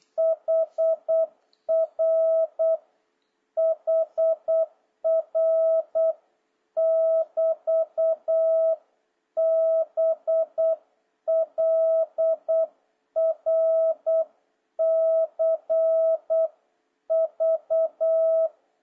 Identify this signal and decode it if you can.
Morse code, heard as a keyed tone, about 8 wpm, 650 Hz. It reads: HR HR <BT> BLRCV
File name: jh8cbh / 2026cw